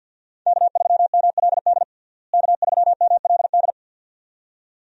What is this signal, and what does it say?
Morse code, heard as a keyed tone, about 52 wpm, 700 Hz. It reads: K3MLD K3MLD
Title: USA Call Signs - 5 Characters - Repeated 2x - 50wpm